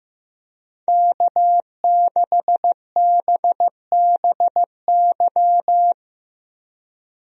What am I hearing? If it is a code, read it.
Morse code, heard as a keyed tone, about 15 wpm, 700 Hz. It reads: K6BBY
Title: USA Call Signs - 5 Characters - Repeated 2x - 15wpm